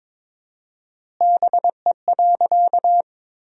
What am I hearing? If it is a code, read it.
Morse code, heard as a keyed tone, about 22 wpm, 700 Hz. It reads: BE.
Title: 4x Word-Spacing - Sentences from Top 200 Words - 22wpm